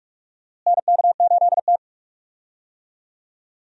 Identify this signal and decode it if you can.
Morse code, heard as a keyed tone, about 46 wpm, 700 Hz. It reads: NK8T
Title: USA Call Signs - 4 Characters - Repeated 3x - 45wpm